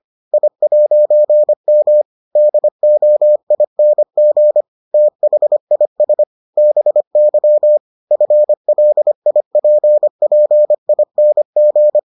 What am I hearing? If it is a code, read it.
Morse code, heard as a keyed tone, about 25 wpm, 600 Hz. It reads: I'M DOING THIS BY FLIPPING